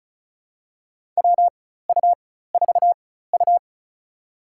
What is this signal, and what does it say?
Morse code, heard as a keyed tone, about 36 wpm, spaced as for 16 wpm, 700 Hz. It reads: WU4U
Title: USA Call Signs - 4 Characters - 4x Character Spacing - 35wpm